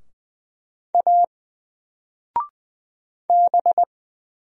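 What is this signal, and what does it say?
Morse code, heard as a keyed tone, about 20 wpm, 700 Hz. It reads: A B
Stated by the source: Single letter and number Morse drill at 20 wpm, from N4LCD.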